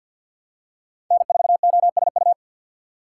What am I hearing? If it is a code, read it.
Morse code, heard as a keyed tone, about 53 wpm, 700 Hz. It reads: N4OSU
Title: USA Call Signs - 5 Characters - 2x Character Spacing - 50wpm